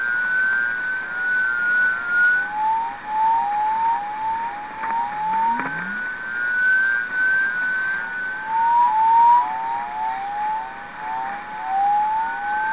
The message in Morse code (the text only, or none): none